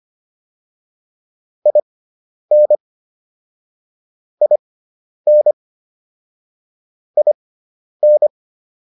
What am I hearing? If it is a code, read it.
Morse code, heard as a keyed tone, about 25 wpm, spaced as for 10 wpm, 600 Hz. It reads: IN IN IN